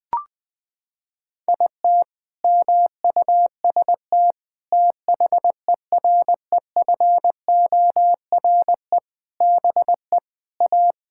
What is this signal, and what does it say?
Morse code, heard as a keyed tone, about 21 wpm, 700 Hz. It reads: IT MUST THEREFORE BE A
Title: Sentences from Top 1000 Words - Review Part 6 - 20wpm